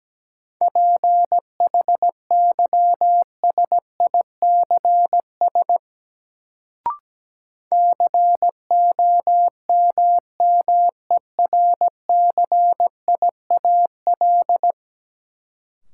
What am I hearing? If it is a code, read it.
Morse code, heard as a keyed tone, about 17 wpm, 700 Hz. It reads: PHYSICS COMMERCIAL
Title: Sentences from Top 1000 Words - Review Part 5 - New Word Drill - 17wpm